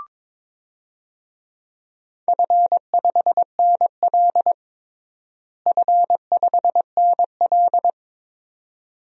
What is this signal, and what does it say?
Morse code, heard as a keyed tone, about 22 wpm, 700 Hz. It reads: F5NL F5NL
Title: DXCC-100 Call Signs - 4 Characters - Repeated 2x - 22wpm